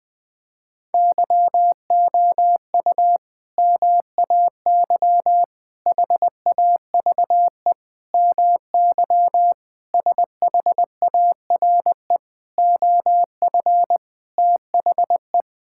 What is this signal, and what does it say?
Morse code, heard as a keyed tone, about 21 wpm, 700 Hz. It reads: YOU MAY HAVE MY SHARE OF THE
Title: Sentences from Top 1000 Words - Review Part 6 - 20wpm